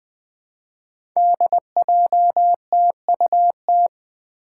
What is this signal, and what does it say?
Morse code, heard as a keyed tone, about 20 wpm, 700 Hz. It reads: DJTUT
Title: Five Challenging Letter-Number Combinations 20wpm